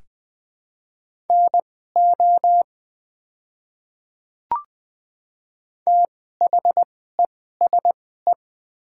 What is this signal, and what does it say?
Morse code, heard as a keyed tone, about 20 wpm, spaced as for 14 wpm, 700 Hz. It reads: NO THESE
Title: 2x Character Spacing - Top 100 Words - 20wpm